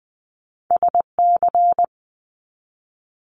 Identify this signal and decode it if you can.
Morse code, heard as a keyed tone, about 20 wpm, 700 Hz. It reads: SC
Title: ARRL Sections 20wpm